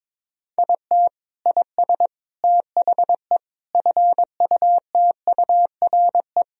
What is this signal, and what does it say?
Morse code, heard as a keyed tone, about 21 wpm, 700 Hz. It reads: IT IS THE FUTURE
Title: Sentences from Top 1000 Words - Review Part 4 - 22wpm